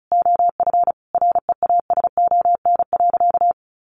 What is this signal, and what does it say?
Morse code, heard as a keyed tone, about 34 wpm, 700 Hz. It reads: OF REASON.
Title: Sentences from Top 1300 Words 35wpm